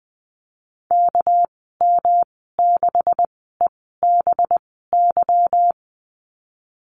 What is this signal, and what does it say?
Morse code, heard as a keyed tone, about 20 wpm, spaced as for 14 wpm, 700 Hz. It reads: KM6EBY
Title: 213 Lesson 34g - Number 8 - Call Signs 20wpm